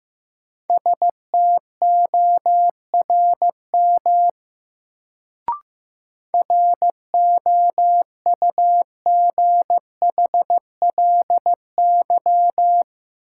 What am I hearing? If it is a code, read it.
Morse code, heard as a keyed tone, about 15 wpm, 700 Hz. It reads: STORM ROUGHLY
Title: Sentences from Top 1500 Words - New Word Drill - 15wpm